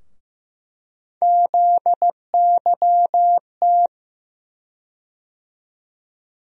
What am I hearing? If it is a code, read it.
Morse code, heard as a keyed tone, about 15 wpm, 700 Hz. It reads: ZYT
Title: USA Call Sign Postfixes - 3 Letters 15wpm